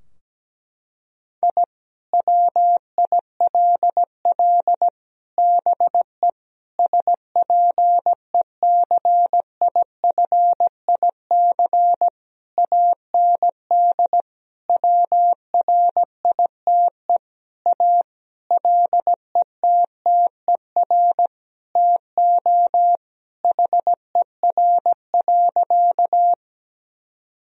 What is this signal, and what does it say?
Morse code, heard as a keyed tone, about 17 wpm, 700 Hz. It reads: I WILL BE SPECIFIC AND WRITE A LETTER TO HER.
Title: Sentences from Top 1000 Words - Review Part 3 - 17wpm